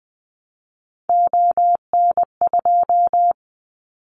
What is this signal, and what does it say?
Morse code, heard as a keyed tone, about 19 wpm, 700 Hz. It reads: ON2
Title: Callsign Prefixes for Top 50 DXCC - 20wpm